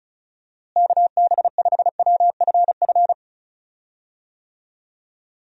Morse code, heard as a keyed tone, about 34 wpm, 700 Hz. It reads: KB5WFF
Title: USA Call Signs - 6 Characters - Repeated 3x - 35wpm